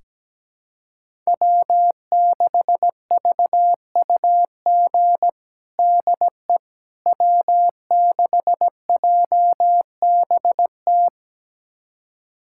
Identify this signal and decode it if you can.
Morse code, heard as a keyed tone, about 17 wpm, 700 Hz. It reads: W6VUG DE W6JBT
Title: USA Callsign DE Callsign - 5 Characters - 17wpm